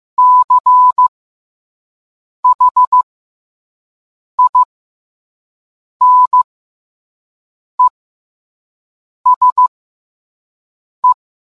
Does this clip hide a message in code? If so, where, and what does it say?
Yes: Morse code from 0.2 to 11.1 s, heard as a keyed tone, about 15 wpm, spaced as for 5 wpm, 1000 Hz. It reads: CHINESE